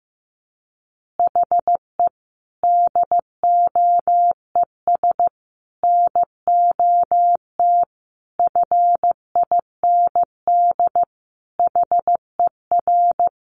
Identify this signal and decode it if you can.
Morse code, heard as a keyed tone, about 15 wpm, 700 Hz. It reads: HE DOES NOT FIND HER
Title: Easy Phrases 15wpm